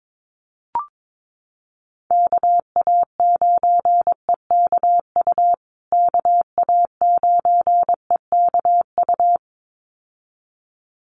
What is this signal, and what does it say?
Morse code, heard as a keyed tone, about 22 wpm, 700 Hz. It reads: KA9EKU KA9EKU
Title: USA Call Signs Repeated 2x 22wpm